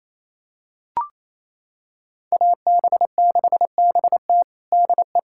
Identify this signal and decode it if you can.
Morse code, heard as a keyed tone, about 28 wpm, 700 Hz. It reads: AB6BT DE
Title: USA Callsign DE Callsign 28wpm